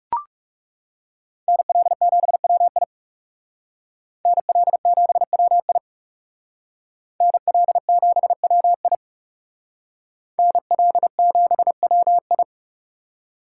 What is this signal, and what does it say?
Morse code, heard as a keyed tone, about 36 wpm, 700 Hz. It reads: NL7WI NL7WI NL7WI NL7WI
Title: Speed Racing - USA 2x2 Call Signs 30wpm